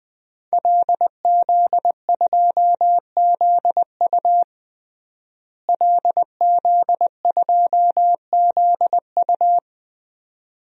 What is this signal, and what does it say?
Morse code, heard as a keyed tone, about 20 wpm, 700 Hz. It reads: LZ2ZU LZ2ZU
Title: DXCC-100 Call Signs - 5 Characters - Repeated 2x - 20wpm